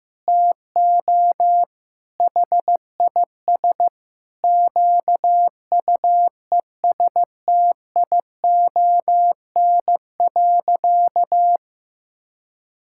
Morse code, heard as a keyed tone, about 16 wpm, 700 Hz. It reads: TO HIS QUESTION.
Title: Sentences from Top 1000 Words - Review Part 6 - 15wpm